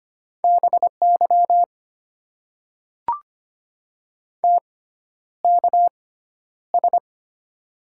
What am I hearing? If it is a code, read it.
Morse code, heard as a keyed tone, about 25 wpm, spaced as for 11 wpm, 700 Hz. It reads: BY TKS
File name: Top 90 QSO Elements - 6x Character Spacing - 25wpm